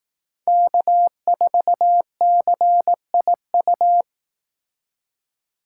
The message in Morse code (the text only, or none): K4CIU